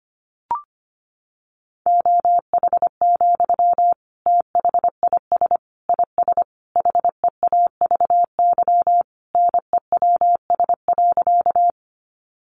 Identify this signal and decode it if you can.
Morse code, heard as a keyed tone, about 25 wpm, 700 Hz. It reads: OH, THIS IS HEAVY NEWS.
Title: Sentences from Top 1100 Words 25wpm